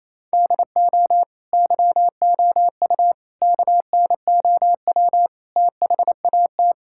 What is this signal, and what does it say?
Morse code, heard as a keyed tone, about 28 wpm, 700 Hz. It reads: DO YOU KNOW THAT
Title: Sentences from Top 1000 Words - Review Part 5 - 28wpm